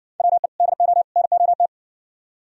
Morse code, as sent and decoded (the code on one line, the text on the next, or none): .-- . / -.. --- / -. --- -
WE DO NOT